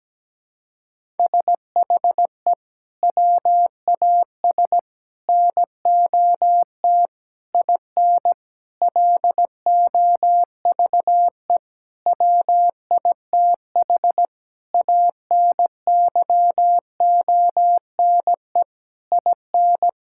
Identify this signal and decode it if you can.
Morse code, heard as a keyed tone, about 17 wpm, 700 Hz. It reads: SHE WAS NOT IN LOVE WITH ANYONE IN
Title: Sentences from Top 1000 Words - Review Part 5 - 17wpm